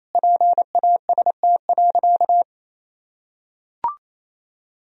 Morse code, heard as a keyed tone, about 28 wpm, 700 Hz. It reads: PAST.
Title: Sentences from Top 1000 Words - Review Part 3 - 28wpm